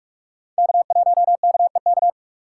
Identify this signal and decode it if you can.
Morse code, heard as a keyed tone, about 47 wpm, 700 Hz. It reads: K1KEK
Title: USA Call Signs - 3x Character Spacing - 45wpm